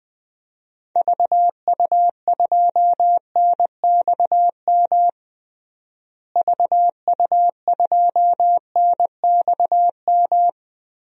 Morse code, heard as a keyed tone, about 20 wpm, 700 Hz. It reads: VU2NXM VU2NXM
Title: DXCC-100 Call Signs - 6 Characters - Repeated 3x - 20wpm